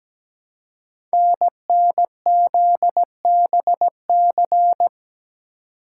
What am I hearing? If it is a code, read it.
Morse code, heard as a keyed tone, about 17 wpm, 700 Hz. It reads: NNZBC
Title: Code Groups - 5 Letters - 17wpm